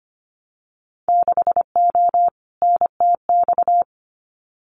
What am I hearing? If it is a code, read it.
Morse code, heard as a keyed tone, about 25 wpm, 700 Hz. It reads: 6O NTX
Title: Winter Field Day Exchanges Encore 25wpm